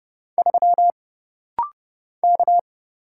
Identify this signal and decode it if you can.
Morse code, heard as a keyed tone, about 30 wpm, 700 Hz. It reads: 3 K
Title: Single Letter-Number - Rapid-Fire 30wpm